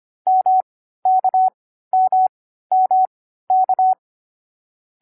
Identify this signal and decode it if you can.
Morse code, heard as a keyed tone, about 25 wpm, spaced as for 14 wpm, 750 Hz. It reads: MKMMK
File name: CW-Koch / koch.2.4.groups5.15wpm